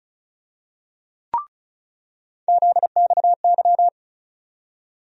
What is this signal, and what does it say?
Morse code, heard as a keyed tone, about 36 wpm, 700 Hz. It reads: ZXY
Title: USA Call Sign Postfixes - 3 Letters 35wpm